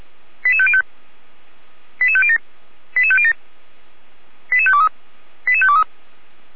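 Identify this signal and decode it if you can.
Five-tone selective call (ZVEI), heard as quick runs of five tones, tones of about 80 ms. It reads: 88505 88507 88507 88523 88523
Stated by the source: ZVEI five-tone selective call 88427 received on an SDR.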